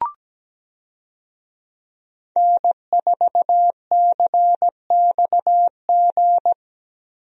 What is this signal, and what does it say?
Morse code, heard as a keyed tone, about 17 wpm, 700 Hz. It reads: N4CXG
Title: USA Call Signs - 5 Characters - Repeated 3x - 17wpm